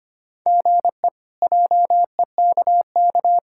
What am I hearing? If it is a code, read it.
Morse code, heard as a keyed tone, about 25 wpm, 700 Hz. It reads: GE JEKK
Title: K1USN SST QSO Practice - 25wpm